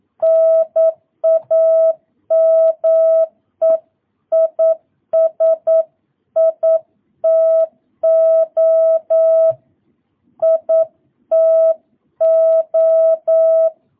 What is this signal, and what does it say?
Morse code, heard as a keyed tone, about 9 wpm, 650 Hz. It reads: NAME IS ITO ITO